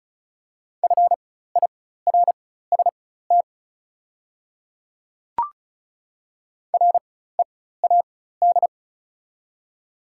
Morse code, heard as a keyed tone, about 36 wpm, spaced as for 16 wpm, 690 Hz. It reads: FIRST READ
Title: 4x Character Spacing - Top 200 Words - 35wpm